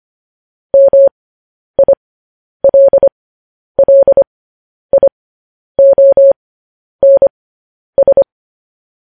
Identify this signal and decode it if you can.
Morse code, heard as a keyed tone, about 25 wpm, spaced as for 10 wpm, 550 Hz. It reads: MILLIONS